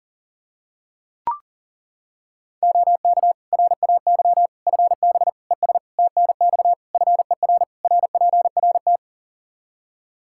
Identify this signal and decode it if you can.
Morse code, heard as a keyed tone, about 40 wpm, 700 Hz. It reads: OK RAY FB ES TNX FER RPRT
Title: Practice QSO - Exchange 2 - Info - 40wpm